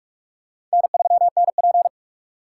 Morse code, heard as a keyed tone, about 43 wpm, 700 Hz. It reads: N3NP
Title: USA Call Signs - 4 Characters - 4x Character Spacing - 45wpm